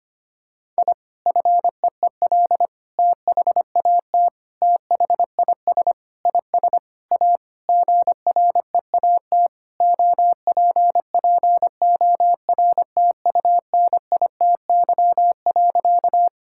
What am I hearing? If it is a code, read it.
Morse code, heard as a keyed tone, about 25 wpm, 700 Hz. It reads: I FEEL THAT THIS IS A GREAT OPPORTUNITY.
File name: practice / 25wpm / Sentences with 4-Word Phrases - 25wpm